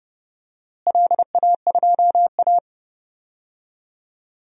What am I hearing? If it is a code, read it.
Morse code, heard as a keyed tone, about 31 wpm, 700 Hz. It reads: LA2A
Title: DXCC-100 Call Signs - 4 Characters - Repeated 2x - 30wpm